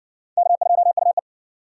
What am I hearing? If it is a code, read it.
Morse code, heard as a keyed tone, about 60 wpm, 700 Hz. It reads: K2UE